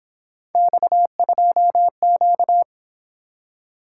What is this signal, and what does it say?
Morse code, heard as a keyed tone, about 26 wpm, 700 Hz. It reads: X2Q
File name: Speed-Racing - Three Letter-Number Combinations - 22wpm